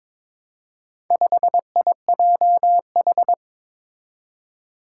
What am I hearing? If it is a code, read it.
Morse code, heard as a keyed tone, about 23 wpm, 700 Hz. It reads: 5IJH
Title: Four Challenging Letter-Number Combinations 22wpm